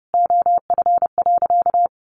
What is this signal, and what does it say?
Morse code, heard as a keyed tone, about 30 wpm, 700 Hz. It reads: OF.
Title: A Sound Of Thunder 30wpm